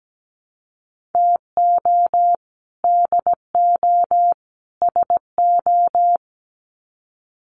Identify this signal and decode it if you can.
Morse code, heard as a keyed tone, about 17 wpm, 700 Hz. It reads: TO DO SO
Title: Sets of 3 Words 17wpm